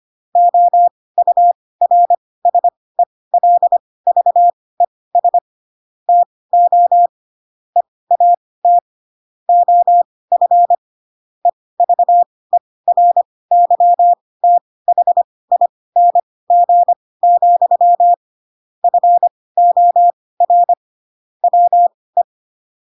Morse code, heard as a keyed tone, about 25 wpm, spaced as for 18 wpm, 700 Hz. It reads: OURSELVES TO EAT OF EVERYTHING, FOR WE
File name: Pinocchio / Pinocchio ch7 12 25 ik1hge